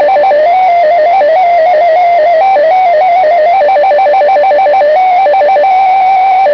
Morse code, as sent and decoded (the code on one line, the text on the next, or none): none